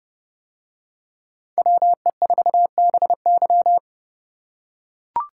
Morse code, heard as a keyed tone, about 28 wpm, 700 Hz. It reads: WE4BY